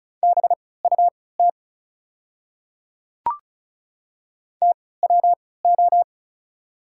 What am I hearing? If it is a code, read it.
Morse code, heard as a keyed tone, about 34 wpm, spaced as for 20 wpm, 700 Hz. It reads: BUT TWO
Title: 3x Character Spacing - Top 200 Words - 35wpm